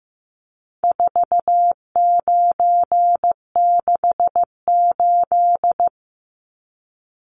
Morse code, heard as a keyed tone, about 15 wpm, 700 Hz. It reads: 4968
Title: Numbers - 4-Digit 15wpm